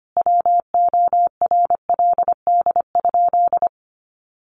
Morse code, heard as a keyed tone, about 25 wpm, 700 Hz. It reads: WORLD?
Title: Sentences from Top 900 Words 25wpm